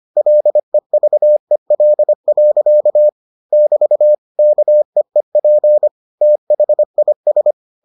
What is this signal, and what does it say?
Morse code, heard as a keyed tone, about 25 wpm, 600 Hz. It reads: LEVEL. <BT> KEEP THIS